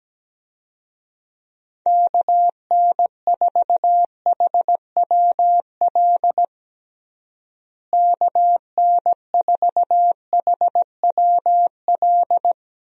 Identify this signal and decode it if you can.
Morse code, heard as a keyed tone, about 17 wpm, 700 Hz. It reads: KN4HWL KN4HWL
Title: USA Call Signs - 6 Characters - Repeated 3x - 17wpm